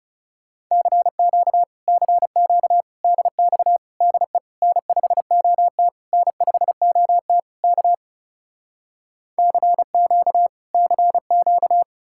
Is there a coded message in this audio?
Yes: Morse code from 0.7 to 11.8 s, heard as a keyed tone, about 40 wpm, 700 Hz. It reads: CQ CQ DX DE N5OT N5OT K CQ CQ